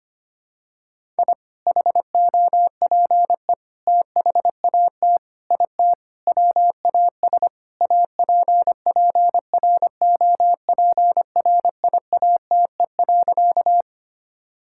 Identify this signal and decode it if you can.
Morse code, heard as a keyed tone, about 25 wpm, 700 Hz. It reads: I HOPE THAT IT WAS APPROPRIATE.
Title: Sentences from Top 1000 Words - Review Part 5 - 25wpm